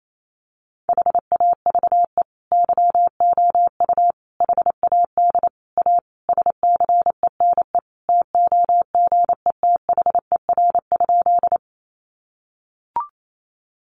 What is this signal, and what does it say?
Morse code, heard as a keyed tone, about 28 wpm, 700 Hz. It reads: HAVE YOU HAD A SCENE TOGETHER?
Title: Sentences from Top 1100 Words 28wpm